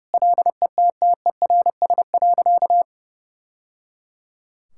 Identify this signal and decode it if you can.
Morse code, heard as a keyed tone, about 30 wpm, 700 Hz. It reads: LETTERS.
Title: Sentences from Top 1000 Words - Review Part 6 - 30wpm